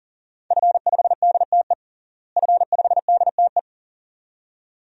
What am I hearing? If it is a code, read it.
Morse code, heard as a keyed tone, about 40 wpm, 700 Hz. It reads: F5DTE F5DTE
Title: DXCC-100 Call Signs - 5 Characters - Repeated 2x - 40wpm